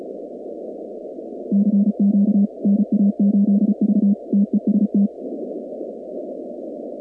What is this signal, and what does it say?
Morse code, heard as a keyed tone, about 34 wpm, 200 Hz. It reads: CQ NA8V TEST
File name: contests / 2019-cw-wpx-kt5j-k5pi- -2330 0-46